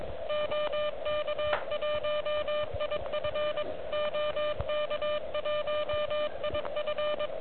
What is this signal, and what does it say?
Morse code, heard as a keyed tone, about 22 wpm, 600 Hz, with noise. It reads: OK1IF OK1IF
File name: OK QRP 2022 / Zvuk / REC37